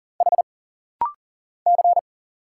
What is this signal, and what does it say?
Morse code, heard as a keyed tone, about 39 wpm, 700 Hz. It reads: H C
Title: Single Letter-Number Alt - Rapid-Fire 40wpm